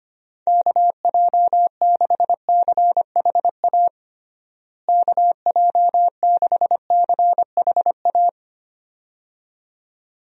Morse code, heard as a keyed tone, about 25 wpm, 700 Hz. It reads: KJ6CHA KJ6CHA
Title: USA Call Signs - 6 Characters - Repeated 2x - 25wpm